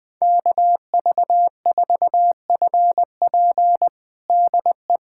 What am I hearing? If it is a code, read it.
Morse code, heard as a keyed tone, about 20 wpm, 700 Hz. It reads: KV4FP DE
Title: USA Callsign DE Callsign - 5 Characters - 20wpm